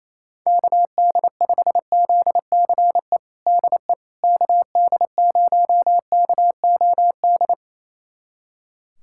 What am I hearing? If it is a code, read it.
Morse code, heard as a keyed tone, about 28 wpm, 700 Hz. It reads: KD5ZCE DE KD0KOD
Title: USA Callsign DE Callsign - 6 Characters - 28wpm